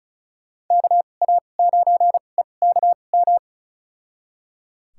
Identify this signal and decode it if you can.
Morse code, heard as a keyed tone, about 34 wpm, spaced as for 25 wpm, 700 Hz. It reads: KA9EKM
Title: USA Call Signs - 2x Character Spacing - 35wpm